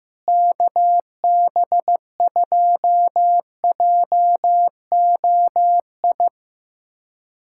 Morse code, heard as a keyed tone, about 15 wpm, 700 Hz. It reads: KB2JOI
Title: USA Callsign DE Callsign - 6 Characters - 15wpm